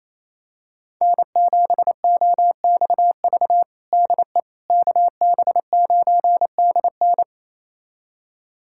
Morse code, heard as a keyed tone, about 28 wpm, 700 Hz. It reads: N7OXV DE KB9DN
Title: USA Callsign DE Callsign - 5 Characters - 28wpm